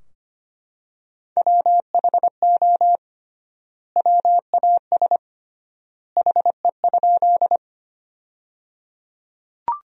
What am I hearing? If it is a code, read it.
Morse code, heard as a keyed tone, about 25 wpm, 700 Hz. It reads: WHO WAS HE?
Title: 3x Word-Spacing - Sentences from Top 100 Words - 25wpm